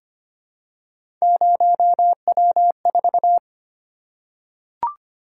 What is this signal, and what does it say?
Morse code, heard as a keyed tone, about 25 wpm, 700 Hz. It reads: 0W4
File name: Three Letter-Number Combinations - 25wpm